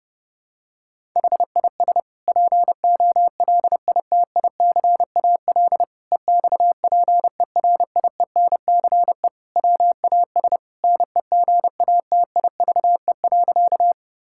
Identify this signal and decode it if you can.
Morse code, heard as a keyed tone, about 30 wpm, 700 Hz. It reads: HIS POLITICAL EXPERIENCE WAS NEGATIVE.
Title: Sentences from Top 1000 Words - Review Part 6 - 30wpm